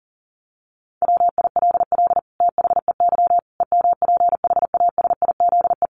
Morse code, heard as a keyed tone, about 40 wpm, 700 Hz. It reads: WILL THEY EMPHASIZE
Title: Sentences from Top 1500 Words 40wpm